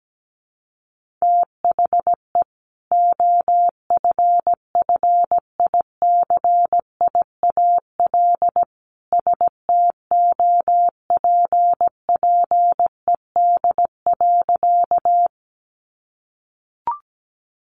Morse code, heard as a keyed tone, about 17 wpm, 700 Hz. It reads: THE OFFICIAL STOPPED.
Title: A Sound Of Thunder 17wpm